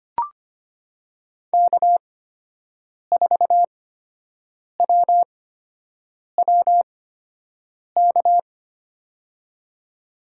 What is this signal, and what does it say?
Morse code, heard as a keyed tone, about 25 wpm, spaced as for 7 wpm, 700 Hz. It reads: K4WWK
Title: USA Call Signs - 5 Characters - 8x Character Spacing - 25wpm